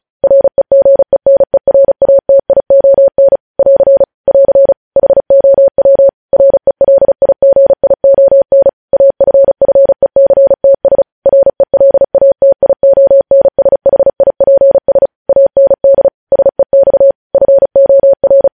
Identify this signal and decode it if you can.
Morse code, heard as a keyed tone, about 36 wpm, 550 Hz. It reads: REGENERATION <AR> <AR> HOW RELIGION AFFECTS RELATIONSHIPS AND SEX FOR
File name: podcast / News headlines-1x-35x35